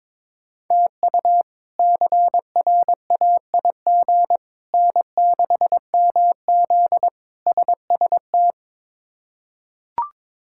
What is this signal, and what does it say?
Morse code, heard as a keyed tone, about 22 wpm, 700 Hz. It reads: TU CRAIG N6MZ SST
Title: K1USN SST QSO Practice - 22wpm